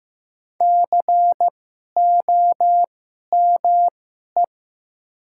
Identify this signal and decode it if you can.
Morse code, heard as a keyed tone, about 15 wpm, spaced as for 11 wpm, 700 Hz. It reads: COME
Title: 2x Character Spacing - Top 100 Words - 15wpm